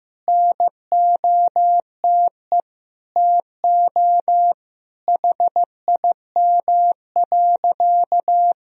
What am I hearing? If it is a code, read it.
Morse code, heard as a keyed tone, about 15 wpm, 700 Hz. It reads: NOTE TO HIM.
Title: Sentences from Top 1000 Words - Review Part 2 - 15wpm